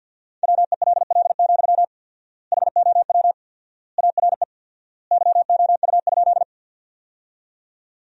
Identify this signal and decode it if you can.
Morse code, heard as a keyed tone, about 49 wpm, 700 Hz. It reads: WELL, HOW ARE YOU?
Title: 4x Word-Spacing - Sentences from Top 200 Words - 50wpm